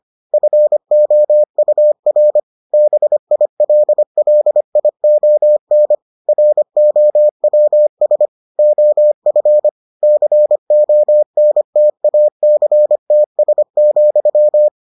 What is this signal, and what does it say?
Morse code, heard as a keyed tone, about 25 wpm, 600 Hz. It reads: FOUR BILLION ROWS OF CONTACTS,